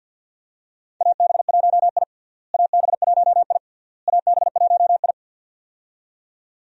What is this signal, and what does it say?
Morse code, heard as a keyed tone, about 52 wpm, 700 Hz. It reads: AB1I AB1I AB1I
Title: USA Call Signs - 4 Characters - Repeated 3x - 50wpm